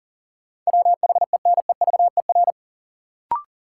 Morse code, heard as a keyed tone, about 40 wpm, 700 Hz. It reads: WHENEVER
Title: Sentences from Top 1200 Words - New Word Drill - 40wpm